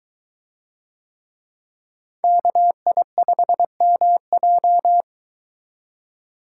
Morse code, heard as a keyed tone, about 22 wpm, 700 Hz. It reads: KI5MJ